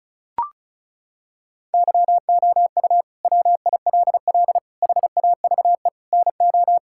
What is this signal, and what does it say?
Morse code, heard as a keyed tone, about 36 wpm, 700 Hz. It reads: YOU WILL HAVE NO
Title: Sentences from Top 1000 Words - Review Part 6 - 35wpm